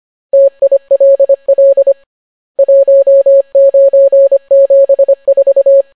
Morse code, heard as a keyed tone, about 25 wpm, 550 Hz. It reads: TILL 1974